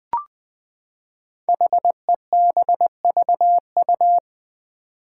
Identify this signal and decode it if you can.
Morse code, heard as a keyed tone, about 20 wpm, 700 Hz. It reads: HEBVU
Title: Five Challenging Letter-Number Combinations 20wpm